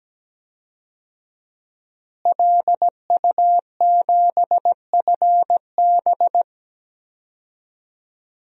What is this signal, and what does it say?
Morse code, heard as a keyed tone, about 17 wpm, 700 Hz. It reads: LU7FB
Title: DXCC-100 Call Signs - 5 Characters - Repeated 3x - 17wpm